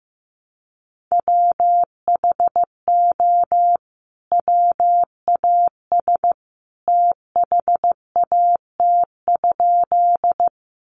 Morse code, heard as a keyed tone, about 15 wpm, 700 Hz. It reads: WHO WAS THAT?
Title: Sentences from Top 100 Words 15wpm